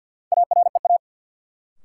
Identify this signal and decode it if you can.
Morse code, heard as a keyed tone, about 53 wpm, 700 Hz. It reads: AREA
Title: Sentences from Top 1000 Words - Review Part 2 - New Word Drill - 50wpm